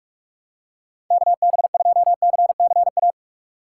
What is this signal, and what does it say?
Morse code, heard as a keyed tone, about 46 wpm, 700 Hz. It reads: KB2CCA
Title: USA Call Signs - 6 Characters - 2x Character Spacing - 45wpm